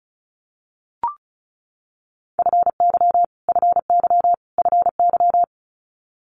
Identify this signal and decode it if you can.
Morse code, heard as a keyed tone, about 36 wpm, 700 Hz. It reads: FY FY FY
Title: Multi-letter Phonemes Repeated 3x 35wpm